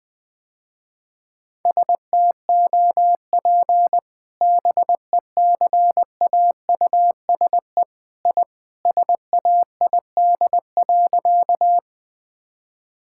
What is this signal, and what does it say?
Morse code, heard as a keyed tone, about 21 wpm, 700 Hz. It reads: STOP BECAUSE I SAID.
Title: Sentences with 2-Word Phrases - Spoken Beginning - 20wpm